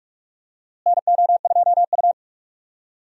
Morse code, heard as a keyed tone, about 46 wpm, 700 Hz. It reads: NO2U
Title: USA Call Signs - 4 Characters - Repeated 3x - 45wpm